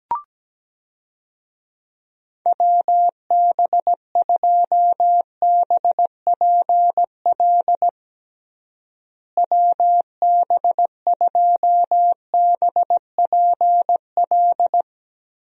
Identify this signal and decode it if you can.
Morse code, heard as a keyed tone, about 17 wpm, 700 Hz. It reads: WB2BPL WB2BPL
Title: USA Call Signs - 6 Characters - Repeated 3x - 17wpm